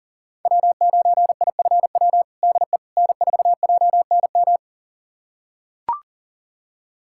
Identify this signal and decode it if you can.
Morse code, heard as a keyed tone, about 40 wpm, 700 Hz. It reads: W9IFW DE N4JNM
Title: USA Callsign DE Callsign - 5 Characters - 40wpm